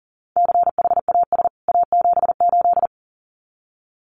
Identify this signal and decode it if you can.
Morse code, heard as a keyed tone, about 40 wpm, 700 Hz. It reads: CHAS A78
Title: CWT Exchange 40wpm